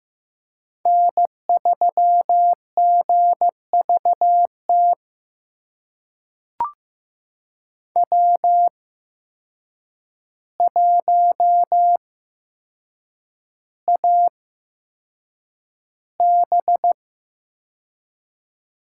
Morse code, heard as a keyed tone, about 15 wpm, spaced as for 7 wpm, 700 Hz. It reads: N3GVT W1AB
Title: USA Call Signs - 5 Characters - 8x Character Spacing - 15wpm